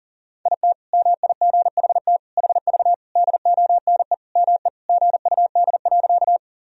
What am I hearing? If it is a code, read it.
Morse code, heard as a keyed tone, about 38 wpm, 700 Hz. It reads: IT MIGHT HV DONE ME GUD.
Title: Sentences from Top 500 Words and Abbreviations 40wpm